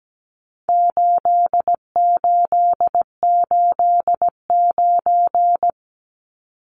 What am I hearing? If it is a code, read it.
Morse code, heard as a keyed tone, about 17 wpm, 700 Hz. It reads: 8889
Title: Numbers - 4-Digit 17wpm